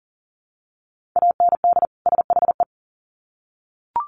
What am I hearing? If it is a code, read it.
Morse code, heard as a keyed tone, about 40 wpm, 700 Hz. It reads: AND SHE
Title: Sets of 2 Words 40wpm